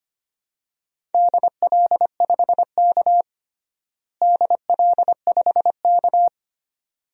Morse code, heard as a keyed tone, about 25 wpm, 700 Hz. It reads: DL5K DL5K
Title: DXCC-100 Call Signs - 4 Characters - Repeated 2x - 25wpm